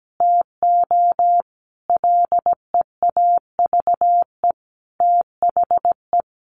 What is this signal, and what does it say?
Morse code, heard as a keyed tone, about 17 wpm, 700 Hz. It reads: TO LEAVE THE
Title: Easy Phrases 17wpm